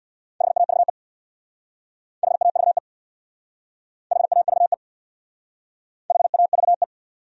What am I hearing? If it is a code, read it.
Morse code, heard as a keyed tone, about 67 wpm, 700 Hz. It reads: HAVE HAVE HAVE HAVE